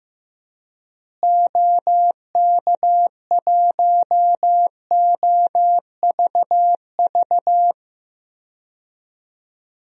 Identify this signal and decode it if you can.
Morse code, heard as a keyed tone, about 15 wpm, 700 Hz. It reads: OK1OVV
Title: DXCC-100 Call Signs - 6 Characters - Repeated 2x - 15wpm